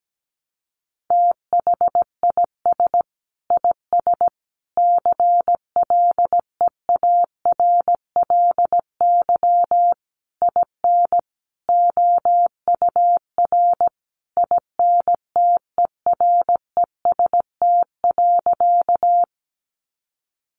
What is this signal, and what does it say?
Morse code, heard as a keyed tone, about 17 wpm, 700 Hz. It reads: THIS IS CLEARLY IN OUR INTEREST.
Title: Sentences from Top 1300 Words 17wpm